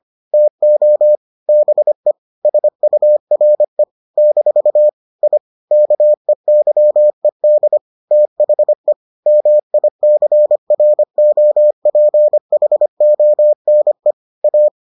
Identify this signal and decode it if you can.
Morse code, heard as a keyed tone, about 25 wpm, 600 Hz. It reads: TO BE SURE - I KEYED THE MICROPHONE A